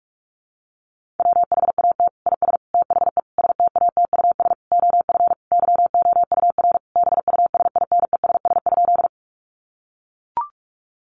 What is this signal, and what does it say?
Morse code, heard as a keyed tone, about 44 wpm, 700 Hz. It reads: WHAT IS THE STATUS OF YOUR BUSINESS?
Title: Sentences from Top 1400 Words 45wpm